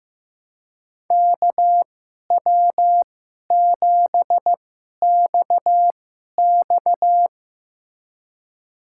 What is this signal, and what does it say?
Morse code, heard as a keyed tone, about 15 wpm, spaced as for 11 wpm, 700 Hz. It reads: KW7XX
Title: USA Call Signs - 5 Characters - 2x Character Spacing - 15wpm